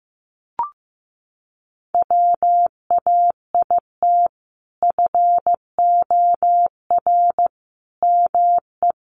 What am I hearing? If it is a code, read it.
Morse code, heard as a keyed tone, about 14 wpm, 700 Hz. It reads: WAIT FOR ME
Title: Sentences from Top 500 Words 15wpm